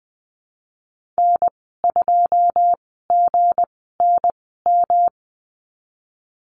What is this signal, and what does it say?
Morse code, heard as a keyed tone, about 20 wpm, spaced as for 15 wpm, 700 Hz. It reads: N2GNM
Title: 167 Lesson 27g - Number 7 - Call Signs 20wpm